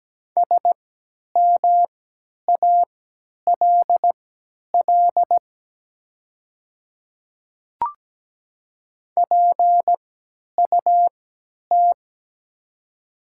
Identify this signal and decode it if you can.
Morse code, heard as a keyed tone, about 17 wpm, spaced as for 10 wpm, 700 Hz. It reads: SMALL PUT
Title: 3x Character Spacing - Top 200 Words - 17wpm